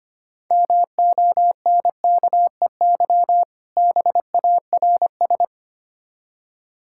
Morse code, heard as a keyed tone, about 26 wpm, 700 Hz. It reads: MONKEY BARS
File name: Easy Words - Speed-Racing with Hints - 25wpm